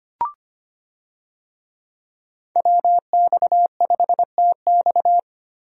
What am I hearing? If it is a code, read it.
Morse code, heard as a keyed tone, about 25 wpm, 700 Hz. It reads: WX5TX
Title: USA Call Signs - 5 Characters - Repeated 3x - 25wpm